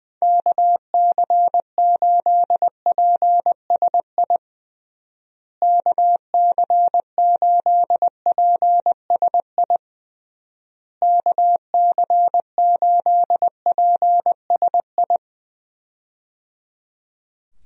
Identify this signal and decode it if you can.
Morse code, heard as a keyed tone, about 20 wpm, 700 Hz. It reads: KC8PSI KC8PSI KC8PSI
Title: USA Call Signs - 6 Characters - Repeated 3x - 20wpm